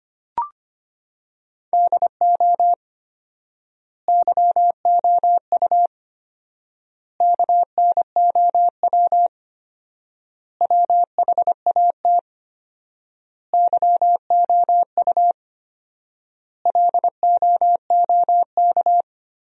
Morse code, heard as a keyed tone, about 25 wpm, 700 Hz. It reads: DO YOU KNOW WHAT YOU LOOK
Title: 4x Word-Spacing - Sentences from Top 100 Words - 25wpm